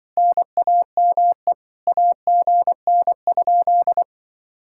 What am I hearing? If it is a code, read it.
Morse code, heard as a keyed tone, about 24 wpm, 700 Hz. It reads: NAME AGN?
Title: Speed-Racing Common QSO Phrases Encore - 20wpm